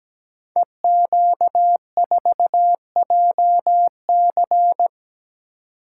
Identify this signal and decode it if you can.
Morse code, heard as a keyed tone, about 17 wpm, 700 Hz. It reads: EQ4JC
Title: Code Groups - 5 Letter:Numbers - 17wpm